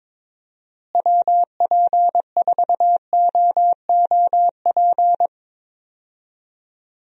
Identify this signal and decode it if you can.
Morse code, heard as a keyed tone, about 22 wpm, 700 Hz. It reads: WP4OOP
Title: USA Call Signs - 6 Characters - Repeated 3x - 22wpm